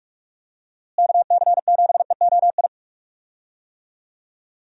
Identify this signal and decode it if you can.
Morse code, heard as a keyed tone, about 44 wpm, 700 Hz. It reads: KC7EOI
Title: USA Call Signs - 6 Characters - Repeated 2x - 45wpm